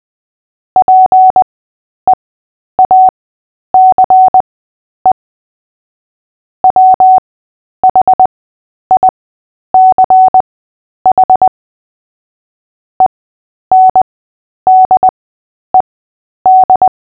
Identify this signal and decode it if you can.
Morse code, heard as a keyed tone, about 20 wpm, spaced as for 10 wpm, 750 Hz. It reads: PEACE WHICH ENDED
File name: Short-Story-7-10wpm